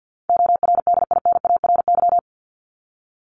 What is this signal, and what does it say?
Morse code, heard as a keyed tone, about 50 wpm, 700 Hz. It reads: ORDINARY
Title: Top 1751-2000 Words 50wpm